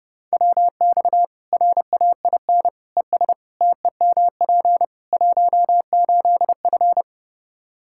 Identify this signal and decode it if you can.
Morse code, heard as a keyed tone, about 30 wpm, 700 Hz. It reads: WX RAIN ES TEMP 18F
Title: K1USN SST QSO Practice 30wpm